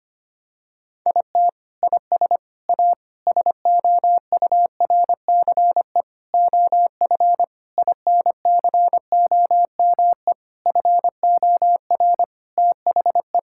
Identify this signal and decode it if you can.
Morse code, heard as a keyed tone, about 25 wpm, 700 Hz. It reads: IT IS A SOURCE OF INCOME FOR THE
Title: Sentences from Top 1000 Words - Review Part 5 - 25wpm